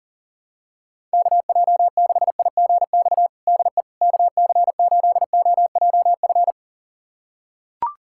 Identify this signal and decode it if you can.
Morse code, heard as a keyed tone, about 40 wpm, 700 Hz. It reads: KJ6IGX DE KC8OJF
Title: USA Callsign DE Callsign - 6 Characters - 40wpm